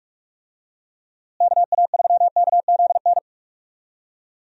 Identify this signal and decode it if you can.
Morse code, heard as a keyed tone, about 46 wpm, 700 Hz. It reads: KA3KZN
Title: USA Call Signs - 6 Characters - Repeated 3x - 45wpm